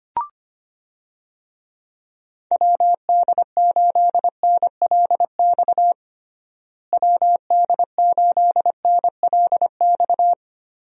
Morse code, heard as a keyed tone, about 25 wpm, 700 Hz. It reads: WD8NLX WD8NLX
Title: USA Call Signs - 6 Characters - Repeated 2x - 25wpm